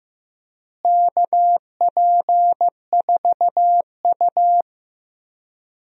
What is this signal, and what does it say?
Morse code, heard as a keyed tone, about 15 wpm, 700 Hz. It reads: KP4U
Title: DXCC-100 Call Signs - 4 Characters - Repeated 2x - 15wpm